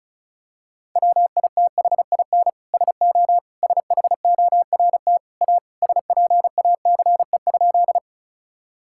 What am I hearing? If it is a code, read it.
Morse code, heard as a keyed tone, about 36 wpm, 700 Hz. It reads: WITHIN SO SHORT A SPACE?
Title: Sentences from Top 1000 Words - Review Part 5 - 35wpm